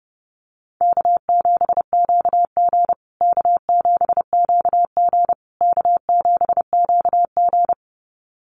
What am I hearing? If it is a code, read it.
Morse code, heard as a keyed tone, about 30 wpm, 700 Hz. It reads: K7QG K7QG K7QG
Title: USA Call Signs Repeated 3x 30wpm